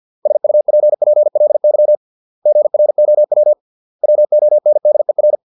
Morse code, heard as a keyed tone, about 50 wpm, 600 Hz. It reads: SUPPLY GROW WONDER
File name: 632 words / 632 Words 50WPM